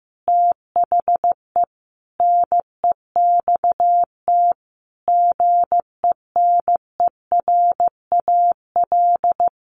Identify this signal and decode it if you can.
Morse code, heard as a keyed tone, about 14 wpm, 700 Hz. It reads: THE NEXT GENERAL
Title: Sentences from Top 1500 Words 15wpm